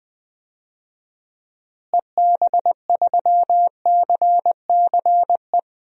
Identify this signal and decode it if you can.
Morse code, heard as a keyed tone, about 21 wpm, 700 Hz. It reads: EB3CCE
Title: DXCC-100 Call Signs - 6 Characters - Repeated 3x - 20wpm